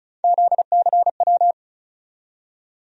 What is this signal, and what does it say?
Morse code, heard as a keyed tone, about 36 wpm, 700 Hz. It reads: ZCW
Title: USA Call Sign Postfixes - 3 Letters 35wpm